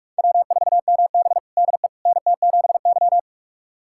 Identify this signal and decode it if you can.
Morse code, heard as a keyed tone, about 42 wpm, 700 Hz. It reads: W4MB DE NT7Y
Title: USA Callsign DE Callsign - 4 Characters - 45wpm